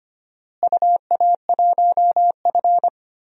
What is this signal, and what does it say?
Morse code, heard as a keyed tone, about 25 wpm, 700 Hz. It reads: UA1F
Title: DXCC-100 Call Signs - 4 Characters - Repeated 3x - 25wpm